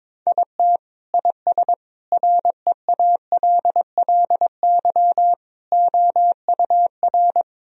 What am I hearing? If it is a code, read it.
Morse code, heard as a keyed tone, about 21 wpm, 700 Hz. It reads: IT IS REALLY OUR
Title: Sentences from Top 1000 Words - Review Part 2 - 22wpm